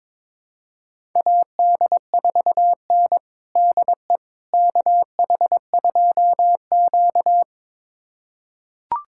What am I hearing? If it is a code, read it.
Morse code, heard as a keyed tone, about 22 wpm, 700 Hz. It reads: AD4N DE KH2Q
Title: USA Callsign DE Callsign - 4 Characters - 22wpm